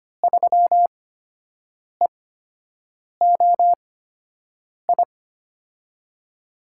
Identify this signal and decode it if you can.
Morse code, heard as a keyed tone, about 25 wpm, spaced as for 7 wpm, 700 Hz. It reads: 3EOI
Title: USA Call Signs - 6 Characters - 8x Character Spacing - 25wpm